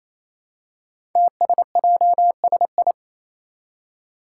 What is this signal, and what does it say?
Morse code, heard as a keyed tone, about 28 wpm, 700 Hz. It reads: TSJSI
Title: Five Challenging Letter-Number Combinations 28wpm